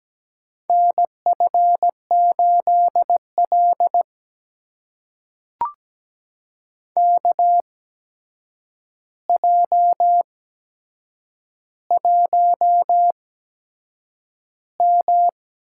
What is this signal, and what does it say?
Morse code, heard as a keyed tone, about 17 wpm, spaced as for 7 wpm, 700 Hz. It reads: NF8L KJ1M
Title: USA Call Signs - 4 Characters - 8x Character Spacing - 17wpm